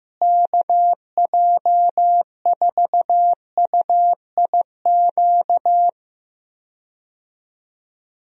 Morse code, heard as a keyed tone, about 15 wpm, 700 Hz. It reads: KJ4UIQ
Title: USA Call Signs - 6 Characters - Repeated 2x - 15wpm